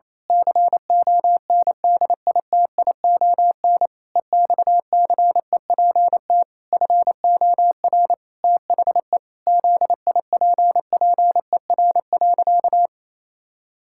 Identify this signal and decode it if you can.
Morse code, heard as a keyed tone, about 28 wpm, 700 Hz. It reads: CONDITION EXCEPT FOR THE ZIPPER.
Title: Pangram Challenge v2 - 28wpm